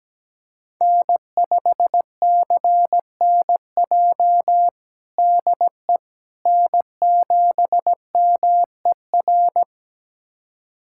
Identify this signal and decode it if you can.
Morse code, heard as a keyed tone, about 17 wpm, 700 Hz. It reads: N5CNJ DE N7MER
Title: USA Callsign DE Callsign - 5 Characters - 17wpm